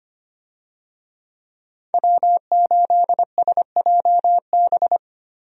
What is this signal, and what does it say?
Morse code, heard as a keyed tone, about 25 wpm, 700 Hz. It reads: W8SJB
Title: USA Call Signs - 5 Characters - Repeated 3x - 25wpm